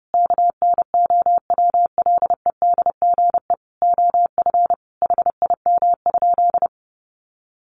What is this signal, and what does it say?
Morse code, heard as a keyed tone, about 30 wpm, 700 Hz. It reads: KNOWLEDGE OF HIM?
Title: Sentences from Top 600 Words 30wpm